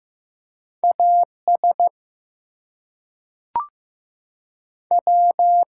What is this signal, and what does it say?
Morse code, heard as a keyed tone, about 15 wpm, 700 Hz. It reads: AS W
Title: Two Letter Words - 15wpm